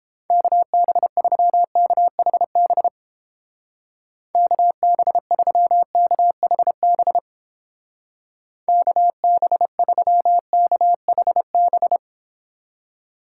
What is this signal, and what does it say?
Morse code, heard as a keyed tone, about 31 wpm, 700 Hz. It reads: KB3KHB KB3KHB KB3KHB